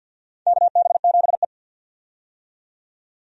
Morse code, heard as a keyed tone, about 52 wpm, 700 Hz. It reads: KB7E
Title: USA Call Signs - 4 Characters - 50wpm